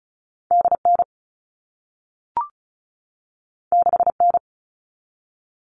Morse code, heard as a keyed tone, about 36 wpm, 700 Hz. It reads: DN 6N
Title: All Cut Numbers - 2-Digit 35wpm